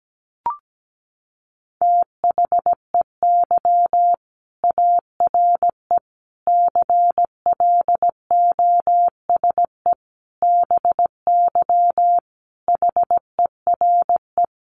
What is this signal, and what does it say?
Morse code, heard as a keyed tone, about 17 wpm, 700 Hz. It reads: THEY ARE CLOSE BY HERE
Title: Sentences from Top 1400 Words 17wpm